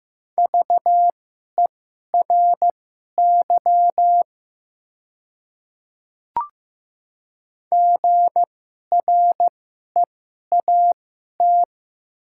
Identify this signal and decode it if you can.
Morse code, heard as a keyed tone, about 15 wpm, spaced as for 11 wpm, 700 Hz. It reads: VERY GREAT
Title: 2x Character Spacing - Top 200 Words - 15wpm